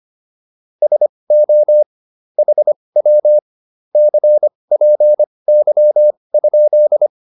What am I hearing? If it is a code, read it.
Morse code, heard as a keyed tone, about 25 wpm, spaced as for 20 wpm, 600 Hz. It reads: SO HW CPY?